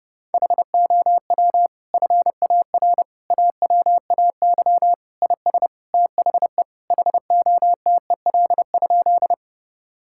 Morse code, heard as a keyed tone, about 32 wpm, 700 Hz. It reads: HOW FAR AWAY IS THE HOTEL?
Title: Sentences from Top 1600 Words 30wpm